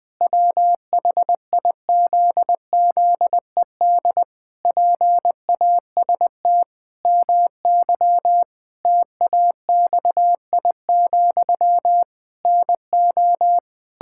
Morse code, heard as a keyed tone, about 20 wpm, 700 Hz. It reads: WHIZZED PAST MY TAXI, NO
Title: Pangram Challenge v2 - 20wpm